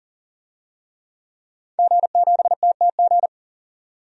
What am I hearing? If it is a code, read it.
Morse code, heard as a keyed tone, about 40 wpm, 700 Hz. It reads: G7TTG
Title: DXCC-100 Call Signs - 5 Characters - 40wpm